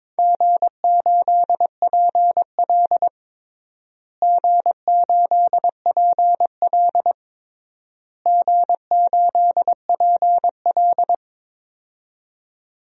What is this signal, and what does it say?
Morse code, heard as a keyed tone, about 22 wpm, 700 Hz. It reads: G8PL G8PL G8PL
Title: DXCC-100 Call Signs - 4 Characters - Repeated 3x - 22wpm